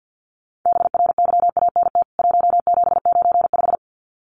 Morse code, heard as a keyed tone, about 49 wpm, 700 Hz. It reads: BRYANT 1795
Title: CWT Exchange 50wpm